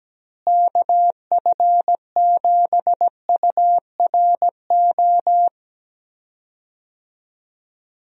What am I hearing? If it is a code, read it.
Morse code, heard as a keyed tone, about 17 wpm, 700 Hz. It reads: KF7URO
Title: USA Call Signs - 6 Characters - Repeated 2x - 17wpm